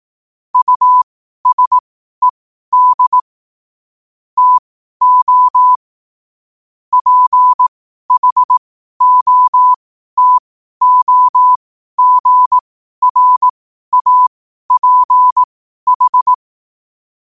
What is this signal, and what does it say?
Morse code, heard as a keyed tone, about 18 wpm, spaced as for 12 wpm, 1000 Hz. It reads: USED TO PHOTOGRAPH
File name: qotdfiles / 12wpm